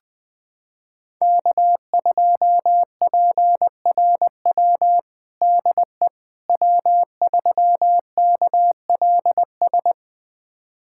Morse code, heard as a keyed tone, about 20 wpm, 700 Hz. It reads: K2PRW DE W3KLS
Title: USA Callsign DE Callsign - 5 Characters - 20wpm